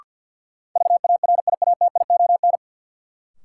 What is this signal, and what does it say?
Morse code, heard as a keyed tone, about 51 wpm, 700 Hz. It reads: VARIATION
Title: Sentences from Top 1600 Words - New Word Drill - 50wpm